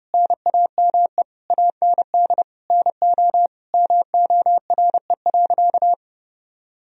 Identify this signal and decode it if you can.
Morse code, heard as a keyed tone, about 32 wpm, 700 Hz. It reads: NAME AND NO MORE.
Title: Sentences from Top 1000 Words - Review Part 1 - 30wpm